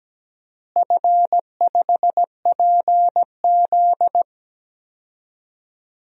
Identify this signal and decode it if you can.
Morse code, heard as a keyed tone, about 17 wpm, 700 Hz. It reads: F5PZ
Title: DXCC-100 Call Signs - 4 Characters - Repeated 3x - 17wpm